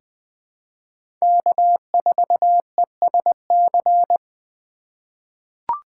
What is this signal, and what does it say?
Morse code, heard as a keyed tone, about 20 wpm, 700 Hz. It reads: K4ESC
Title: USA Call Signs - 5 Characters - 4x Character Spacing - 20wpm